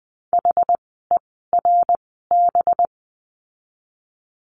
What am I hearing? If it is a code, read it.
Morse code, heard as a keyed tone, about 20 wpm, spaced as for 14 wpm, 700 Hz. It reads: HERB
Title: 185 Lesson 30f - Letter b - Words 20wpm